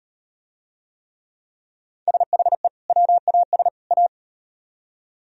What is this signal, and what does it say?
Morse code, heard as a keyed tone, about 38 wpm, 700 Hz. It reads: SHE WAS A